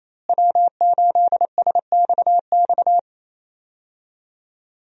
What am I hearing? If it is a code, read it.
Morse code, heard as a keyed tone, about 27 wpm, 700 Hz. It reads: W8SXX
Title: USA Call Signs - 5 Characters - Repeated 3x - 28wpm